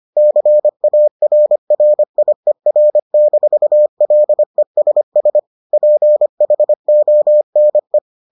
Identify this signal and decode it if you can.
Morse code, heard as a keyed tone, about 25 wpm, 600 Hz. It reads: CARRIER-LESS PHONE